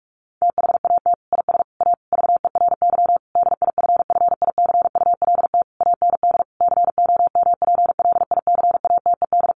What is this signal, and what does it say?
Morse code, heard as a keyed tone, about 44 wpm, 700 Hz. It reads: THAT IS A VERY DIFFICULT AND COMPLICATED
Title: Sentences from Top 1200 Words 45wpm